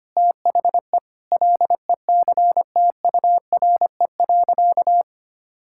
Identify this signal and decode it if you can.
Morse code, heard as a keyed tone, about 25 wpm, 700 Hz. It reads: THE LECTURE.
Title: Sentences from Top 1600 Words 25wpm